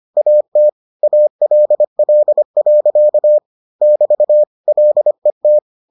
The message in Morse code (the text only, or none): AT ALL. <BT> LET